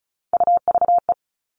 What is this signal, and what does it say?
Morse code, heard as a keyed tone, about 34 wpm, 700 Hz. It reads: UVE
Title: All Cut Numbers - 3-Digit 35wpm